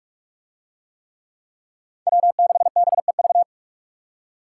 Morse code, heard as a keyed tone, about 42 wpm, 700 Hz. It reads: W6BEV